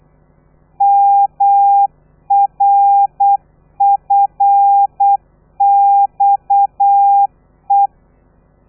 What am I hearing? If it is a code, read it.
Morse code, heard as a keyed tone, about 8 wpm, 800 Hz. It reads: MRFXE
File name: telegrafia / i2tfj / Lez13 35